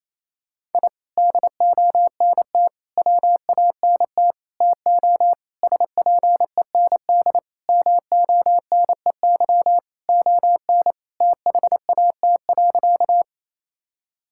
Morse code, heard as a keyed tone, about 28 wpm, 700 Hz. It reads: I DONT WANT TO SPEND MONEY ON THAT.
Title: Sentences from Top 1000 Words - Review Part 5 - 28wpm